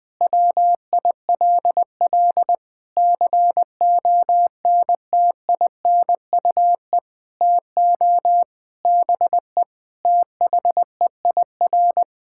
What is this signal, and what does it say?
Morse code, heard as a keyed tone, about 20 wpm, 700 Hz. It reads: WILL CONTINUE TO BE THEIR
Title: Sentences from Top 1600 Words 20wpm